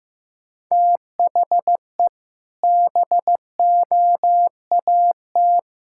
Morse code, heard as a keyed tone, about 16 wpm, 700 Hz. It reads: THE BOAT
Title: Sentences from Top 1000 Words - Review Part 5 - 15wpm